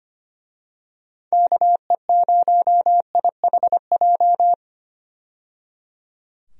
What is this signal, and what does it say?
Morse code, heard as a keyed tone, about 25 wpm, 700 Hz. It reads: KE0IHJ
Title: USA Call Signs - 6 Characters - 25wpm